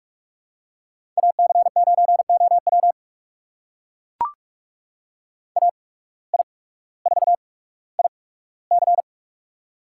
Morse code, heard as a keyed tone, about 44 wpm, spaced as for 19 wpm, 700 Hz. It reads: AC9OW AI4IC
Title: USA Call Signs - 5 Characters - 8x Character Spacing - 45wpm